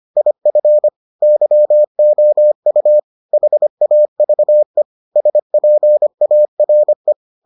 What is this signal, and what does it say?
Morse code, heard as a keyed tone, about 25 wpm, 600 Hz. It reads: IF YOU HAVE SPARE